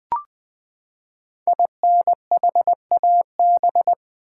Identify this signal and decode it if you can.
Morse code, heard as a keyed tone, about 20 wpm, 700 Hz. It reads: INHAB
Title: Build A Word - 5 Letters At A Time - 10-Letter Words - 20wpm